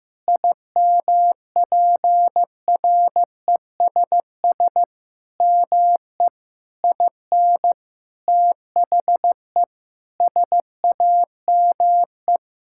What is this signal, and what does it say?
Morse code, heard as a keyed tone, about 15 wpm, 700 Hz. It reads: IMPRESS ME IN THE SAME
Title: Sentences from Top 1600 Words 15wpm